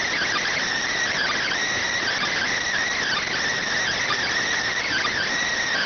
Emergency Alert System (SAME) tones, heard as harsh buzzing bursts. It is heard from 1.1 to 5.3 s.